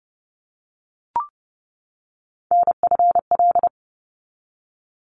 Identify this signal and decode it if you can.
Morse code, heard as a keyed tone, about 30 wpm, 700 Hz. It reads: NFL
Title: ARRL Sections 30wpm